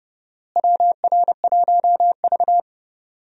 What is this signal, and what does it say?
Morse code, heard as a keyed tone, about 30 wpm, 700 Hz. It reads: WR1V
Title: USA Call Signs - 4 Characters - Repeated 2x - 30wpm